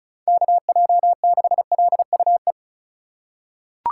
Morse code, heard as a keyed tone, about 36 wpm, 700 Hz. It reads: KJ6LUE
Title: USA Call Signs - 3x Character Spacing - 35wpm